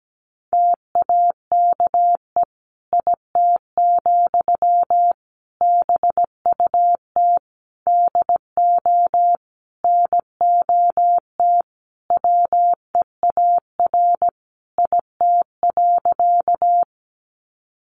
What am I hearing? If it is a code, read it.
Morse code, heard as a keyed tone, about 17 wpm, 700 Hz. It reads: TAKE IT, BUT DO NOT WEAR IT.
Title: Sentences from Top 1500 Words 17wpm